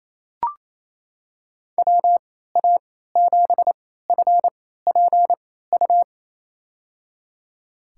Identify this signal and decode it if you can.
Morse code, heard as a keyed tone, about 27 wpm, spaced as for 16 wpm, 700 Hz. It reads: WA7FPU
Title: USA Call Signs - 6 Characters - 3x Character Spacing - 28wpm